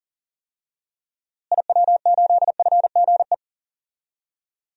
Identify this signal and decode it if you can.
Morse code, heard as a keyed tone, about 38 wpm, 700 Hz. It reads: IW8FGE
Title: DXCC-100 Call Signs - 6 Characters - Repeated 2x - 40wpm